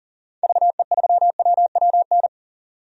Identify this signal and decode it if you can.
Morse code, heard as a keyed tone, about 40 wpm, 700 Hz. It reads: VE3WWN
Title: DXCC-100 Call Signs - 6 Characters - 40wpm